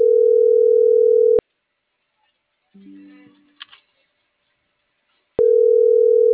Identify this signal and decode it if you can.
Ringback tone (North America), 440 + 480 Hz.